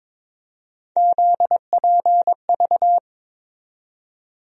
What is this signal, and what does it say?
Morse code, heard as a keyed tone, about 22 wpm, 700 Hz. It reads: ZPV
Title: USA Call Sign Postfixes - 3 Letters 22wpm